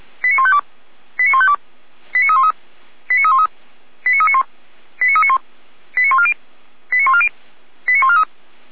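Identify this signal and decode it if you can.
Five-tone selective call (ZVEI), heard as quick runs of five tones, tones of about 80 ms. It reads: 89152 89152 89313 89313 89391 89391 89140 89140 89143
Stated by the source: Five-tone ZVEI selective call received on SDR, from the Feuerwehr Orlamünde (fire brigade) audio server.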